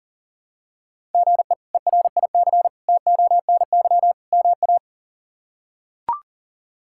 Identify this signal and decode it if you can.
Morse code, heard as a keyed tone, about 40 wpm, 700 Hz. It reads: GE ERIC TONY MA
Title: K1USN SST QSO Practice - 40wpm